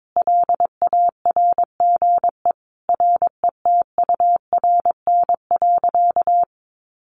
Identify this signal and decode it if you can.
Morse code, heard as a keyed tone, about 22 wpm, 700 Hz. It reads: LARGE RETURN.
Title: Sentences from Top 1100 Words 22wpm